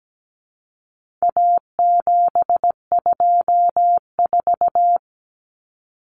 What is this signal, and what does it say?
Morse code, heard as a keyed tone, about 17 wpm, 700 Hz. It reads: A724
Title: Common Cut Numbers - 4-Digit 17wpm